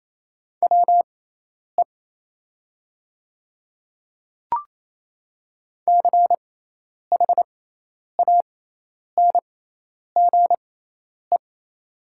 Morse code, heard as a keyed tone, about 27 wpm, spaced as for 10 wpm, 700 Hz. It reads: WE CHANGE
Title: 6x Character Spacing - Top 200 Words - 28wpm